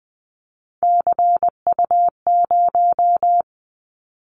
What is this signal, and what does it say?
Morse code, heard as a keyed tone, about 20 wpm, 700 Hz. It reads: CU0
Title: Callsign Prefixes for Top 50 DXCC - 20wpm